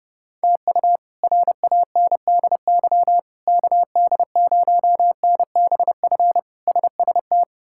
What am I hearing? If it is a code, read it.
Morse code, heard as a keyed tone, about 32 wpm, 700 Hz. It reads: TU RANDY KD0NBF SST
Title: K1USN SST QSO Practice - 30wpm